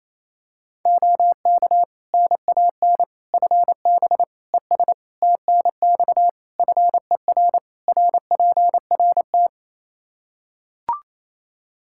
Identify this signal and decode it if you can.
Morse code, heard as a keyed tone, about 28 wpm, 700 Hz. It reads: OK NAN FB ES TNX FER RPRT
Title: Practice QSO - Exchange 2 - Info - 28wpm